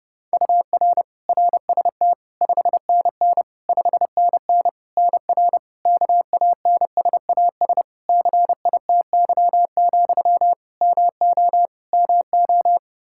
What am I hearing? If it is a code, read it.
Morse code, heard as a keyed tone, about 30 wpm, 700 Hz. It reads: UR RST 5NN 5NN NR KANSAS CITY, MO MO
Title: Practice QSO - Exchange 1 - Beginning - 30wpm